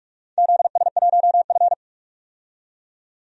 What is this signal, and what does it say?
Morse code, heard as a keyed tone, about 46 wpm, 700 Hz. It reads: ZS1F
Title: DXCC-100 Call Signs - 4 Characters - 45wpm